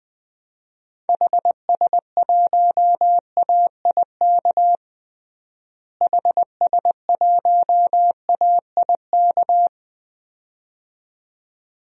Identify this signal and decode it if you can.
Morse code, heard as a keyed tone, about 21 wpm, 700 Hz. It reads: HS1AIK HS1AIK
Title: DXCC-100 Call Signs - 6 Characters - Repeated 2x - 20wpm